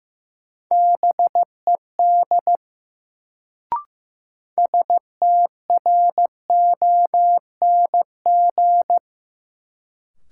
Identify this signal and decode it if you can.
Morse code, heard as a keyed tone, about 16 wpm, 700 Hz. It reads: BED STRONG
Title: Sentences from Top 500 Words - New Word Drill - 15wpm